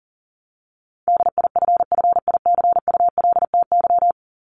Morse code, heard as a keyed tone, about 40 wpm, 700 Hz. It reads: DIFFICULTY
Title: Top 1251-1500 Words 40wpm